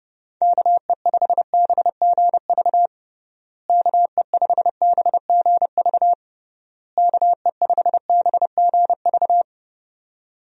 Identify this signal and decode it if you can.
Morse code, heard as a keyed tone, about 32 wpm, 700 Hz. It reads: KE5BGV KE5BGV KE5BGV
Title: USA Call Signs - 6 Characters - Repeated 3x - 30wpm